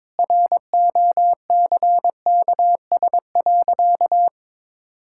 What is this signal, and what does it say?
Morse code, heard as a keyed tone, about 21 wpm, 700 Hz. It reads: ROCKS.
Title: Pangram Challenge v2 - 22wpm